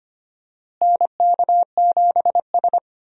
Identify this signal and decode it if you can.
Morse code, heard as a keyed tone, about 25 wpm, 700 Hz. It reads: NK7S
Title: USA Call Signs - 4 Characters - Repeated 3x - 25wpm